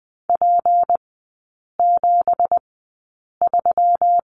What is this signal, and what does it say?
Morse code, heard as a keyed tone, about 20 wpm, spaced as for 8 wpm, 690 Hz. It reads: P73
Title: 198 Lesson 32e - Letter k - Three Characters 20wpm